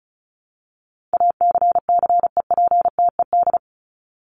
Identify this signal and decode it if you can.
Morse code, heard as a keyed tone, about 34 wpm, 700 Hz. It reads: ACCEPTED
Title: ED Suffix - Top 100 Words with Base Words 35wpm